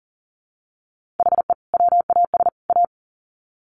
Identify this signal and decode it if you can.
Morse code, heard as a keyed tone, about 40 wpm, 700 Hz. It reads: HE WAS A
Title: Sets of 3 Words 40wpm